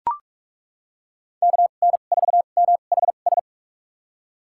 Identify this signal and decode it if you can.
Morse code, heard as a keyed tone, about 41 wpm, spaced as for 31 wpm, 700 Hz. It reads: KN4MHS
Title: USA Call Signs - 6 Characters - 2x Character Spacing - 45wpm